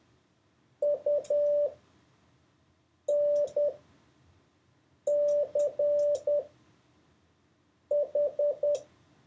Morse code, heard as a keyed tone, about 10 wpm, spaced as for 5 wpm, 580 Hz. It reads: UNCH
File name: 4of6-words-1